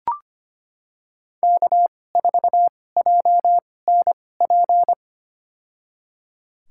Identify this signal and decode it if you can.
Morse code, heard as a keyed tone, about 25 wpm, spaced as for 18 wpm, 700 Hz. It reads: K4JNP
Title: USA Call Signs - 5 Characters - 2x Character Spacing - 25wpm